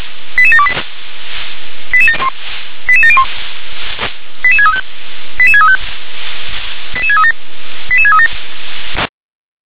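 Five-tone selective call (ZVEI), heard as quick runs of five tones, tones of about 80 ms. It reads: 88728 88701 88635 88635 88637 88637
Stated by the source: ZVEI five-tone selective call received on SDR.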